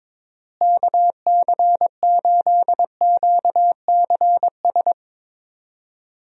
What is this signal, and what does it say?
Morse code, heard as a keyed tone, about 21 wpm, 700 Hz. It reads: KC8QCS
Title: USA Call Signs - 6 Characters - Repeated 3x - 22wpm